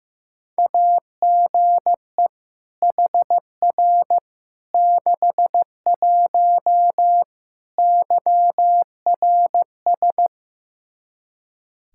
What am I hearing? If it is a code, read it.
Morse code, heard as a keyed tone, about 15 wpm, 700 Hz. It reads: AGE HR 61 YRS
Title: Practice QSO - Exchange 3 - More Info - 15wpm